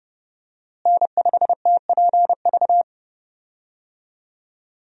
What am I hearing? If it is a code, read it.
Morse code, heard as a keyed tone, about 30 wpm, 700 Hz. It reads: N5TPV
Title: USA Call Signs - 5 Characters - Repeated 2x - 30wpm